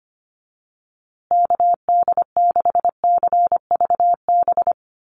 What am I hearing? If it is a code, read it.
Morse code, heard as a keyed tone, about 25 wpm, 700 Hz. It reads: KD6CVB
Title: USA Call Signs Repeated 3x 25wpm